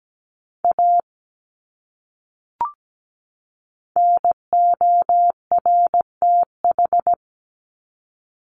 Morse code, heard as a keyed tone, about 17 wpm, 700 Hz. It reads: A NORTH
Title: Top 300 Words 17wpm